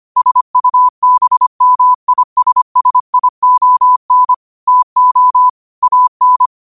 Morse code, heard as a keyed tone, about 25 wpm, 1000 Hz. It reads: SUBMISSION TO AN
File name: qotdfiles / 25wpm